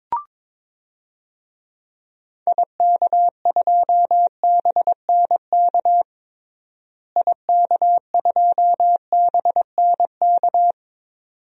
Morse code, heard as a keyed tone, about 22 wpm, 700 Hz. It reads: IK2BNK IK2BNK
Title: DXCC-100 Call Signs - 6 Characters - Repeated 2x - 22wpm